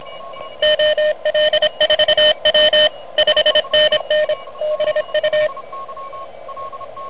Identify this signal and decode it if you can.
Morse code, heard as a keyed tone, about 25 wpm, 610 Hz, with noise. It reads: OL4W 5NN BU
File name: truSDX